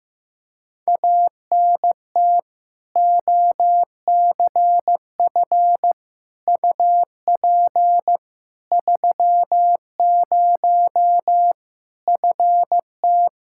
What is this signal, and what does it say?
Morse code, heard as a keyed tone, about 15 wpm, 700 Hz. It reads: ANT OCF UP 30 FT
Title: Practice QSO - Exchange 2 - Info - 15wpm